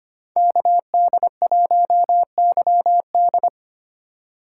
Morse code, heard as a keyed tone, about 25 wpm, 700 Hz. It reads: KD1YD
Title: USA Call Signs - 2x Character Spacing - 25wpm